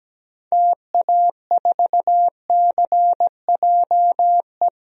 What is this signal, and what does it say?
Morse code, heard as a keyed tone, about 17 wpm, 700 Hz. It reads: TA4CJE
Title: DXCC-100 Call Signs - 6 Characters - 17wpm